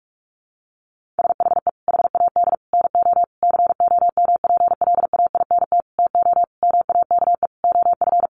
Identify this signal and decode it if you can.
Morse code, heard as a keyed tone, about 44 wpm, 700 Hz. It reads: SHE HAD NO COMPLAINT TO MAKE OF
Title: Sentences from Top 1400 Words 45wpm